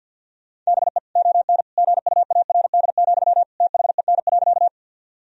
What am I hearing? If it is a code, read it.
Morse code, heard as a keyed tone, about 50 wpm, 700 Hz. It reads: BE ON GUARD, THEN.
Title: Sentences from Top 1000 Words - Review Part 4 - 50wpm